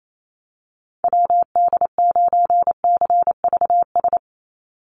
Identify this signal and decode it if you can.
Morse code, heard as a keyed tone, about 28 wpm, 700 Hz. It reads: WD9CVS
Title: USA Call Signs 28wpm